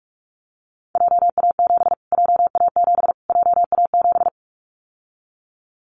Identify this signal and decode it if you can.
Morse code, heard as a keyed tone, about 44 wpm, 700 Hz. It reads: JA7 JA7 JA7
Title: Callsign Prefixes for Top 25 DXCC - Repeated 3x - 45wpm